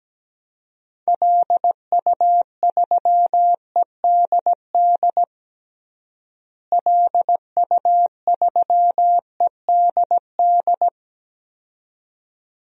Morse code, heard as a keyed tone, about 17 wpm, 700 Hz. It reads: LU3EDD LU3EDD
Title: DXCC-100 Call Signs - 6 Characters - Repeated 2x - 17wpm